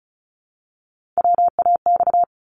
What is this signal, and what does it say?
Morse code, heard as a keyed tone, about 36 wpm, 700 Hz. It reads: WAX
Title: Multi-letter Phonemes Repeated 3x 35wpm